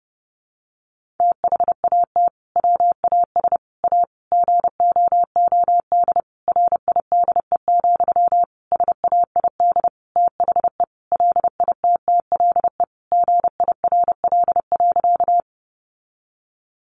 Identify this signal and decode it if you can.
Morse code, heard as a keyed tone, about 30 wpm, 700 Hz. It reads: THAT WAS A GOOD RIDE, SAID THE LITTLE GIRL.
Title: Sentences from Top 1200 Words 30wpm